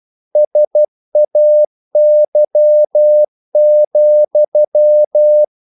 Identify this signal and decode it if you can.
Morse code, heard as a keyed tone, about 12 wpm, 600 Hz. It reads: SAY,
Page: English quotations in CW (telegraphy - morse)